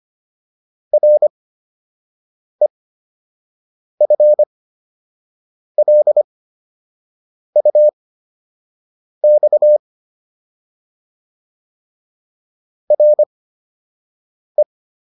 Morse code, heard as a keyed tone, about 25 wpm, spaced as for 6 wpm, 600 Hz. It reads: REFLUX RE